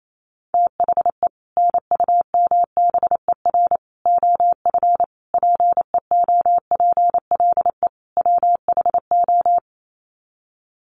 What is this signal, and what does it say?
Morse code, heard as a keyed tone, about 28 wpm, 700 Hz. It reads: THE NUMBER OF PEOPLE WHO
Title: Sets of 5 Words Encore 28wpm